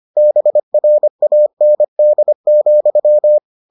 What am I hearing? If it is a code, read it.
Morse code, heard as a keyed tone, about 25 wpm, 600 Hz. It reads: BRAND,